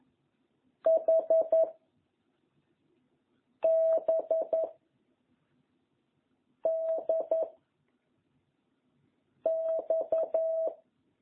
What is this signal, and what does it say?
Morse code, heard as a keyed tone, about 11 wpm, 660 Hz. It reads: H B D X